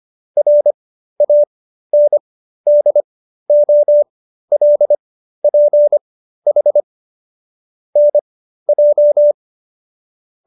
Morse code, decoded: RANDOLPH NJ

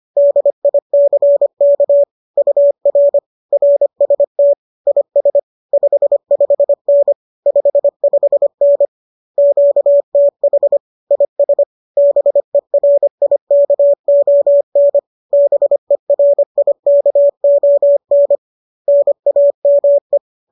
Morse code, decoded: DICK UR RST IS 55N 55N QTH IS BERIKON BERIKON NAME

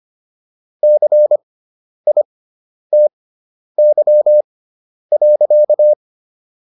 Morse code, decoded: CITY.

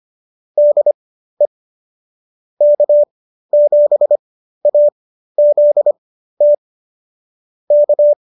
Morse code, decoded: DE K7AZT K

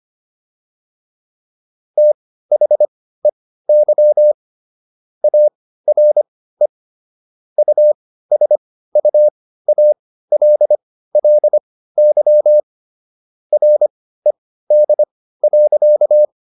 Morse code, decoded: THEY ARE USUALLY RED.

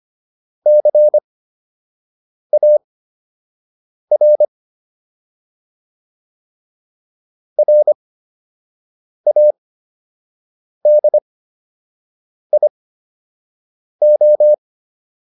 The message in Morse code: CAR RADIO

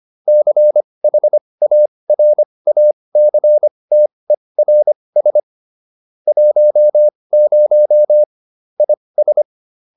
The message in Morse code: CHARACTERS 10 IS

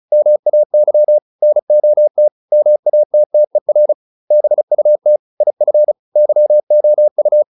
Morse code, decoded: MAY NOT MATTER BUT IF YOU

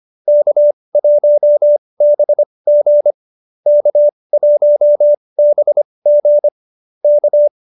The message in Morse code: K1BG K1BG K

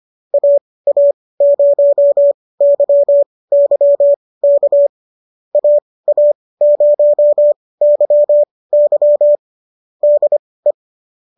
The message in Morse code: AA0YY K AA0YY DE

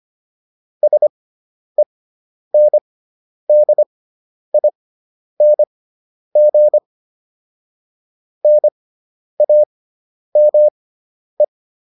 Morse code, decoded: SENDING NAME

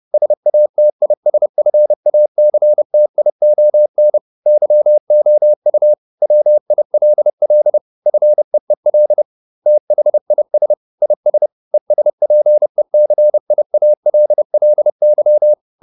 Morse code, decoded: SATISFACTION YOU WILL FEEL THIS IS ESPECIALLY

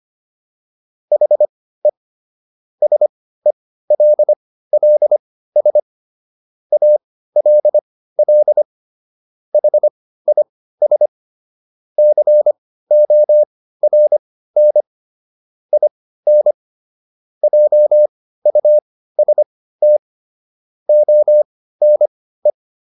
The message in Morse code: HE SELLS ALL HIS CORN IN JUST ONE

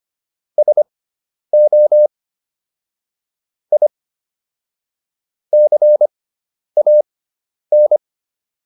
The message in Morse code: SO I CAN

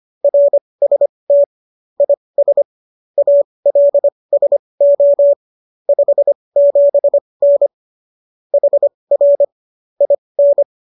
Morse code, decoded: RST IS ALSO 57N HR IN

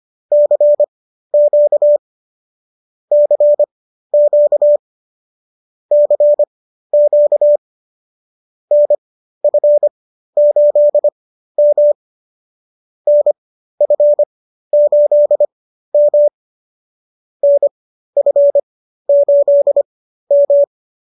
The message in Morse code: CQ CQ CQ NF8M NF8M NF8M

